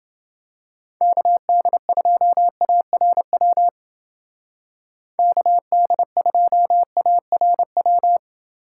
Morse code, heard as a keyed tone, about 29 wpm, 700 Hz. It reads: KD2ARW KD2ARW